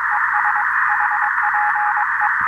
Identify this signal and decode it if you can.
Morse code, heard as a keyed tone, about 22 wpm, 840 Hz, with radio noise. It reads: ES HPE
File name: attachments / 13.03.2015 RV3GM QSO UA1CEG P QRPP